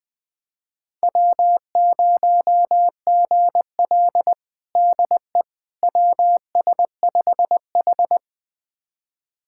Morse code, heard as a keyed tone, about 20 wpm, 700 Hz. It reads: W0GL DE WS5H
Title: USA Callsign DE Callsign - 4 Characters - 20wpm